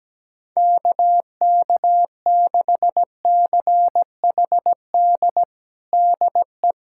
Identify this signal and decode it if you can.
Morse code, heard as a keyed tone, about 17 wpm, 700 Hz. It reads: KK6CHD DE
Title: USA Callsign DE Callsign - 6 Characters - 17wpm